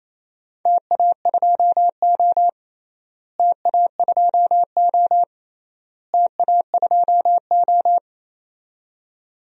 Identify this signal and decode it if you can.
Morse code, heard as a keyed tone, about 28 wpm, 700 Hz. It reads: TA2O TA2O TA2O
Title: DXCC-100 Call Signs - 4 Characters - Repeated 3x - 28wpm